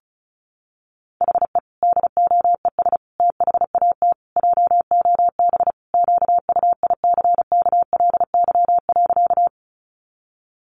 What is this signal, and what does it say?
Morse code, heard as a keyed tone, about 34 wpm, 700 Hz. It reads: HE DOES THAT JOB QUICKLY.
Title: Sentences from Top 700 Words 35wpm